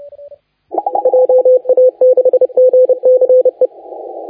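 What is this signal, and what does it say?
Morse code, heard as a keyed tone, about 29 wpm, 510 Hz, with radio noise. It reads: JA6GCE
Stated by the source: Morse contest audio recorded by LA4C during the SAC CW 2018 contest.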